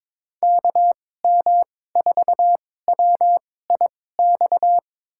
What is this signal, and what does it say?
Morse code, heard as a keyed tone, about 22 wpm, spaced as for 16 wpm, 700 Hz. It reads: KM4WIX
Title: USA Call Signs - 2x Character Spacing - 22wpm